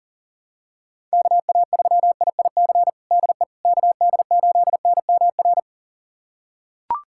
Morse code, heard as a keyed tone, about 40 wpm, 700 Hz. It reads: KA3IIC DE KD8NMR
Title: USA Callsign DE Callsign - 6 Characters - 40wpm